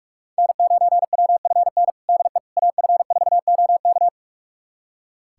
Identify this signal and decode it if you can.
Morse code, heard as a keyed tone, about 41 wpm, 700 Hz. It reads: N9WFN DE AF4OK
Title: USA Callsign DE Callsign - 5 Characters - 45wpm